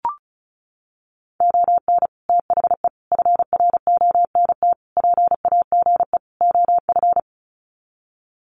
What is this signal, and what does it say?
Morse code, heard as a keyed tone, about 34 wpm, 700 Hz. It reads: ON THE FRONT PAGE OF
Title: Sets of 5 Words Encore 35wpm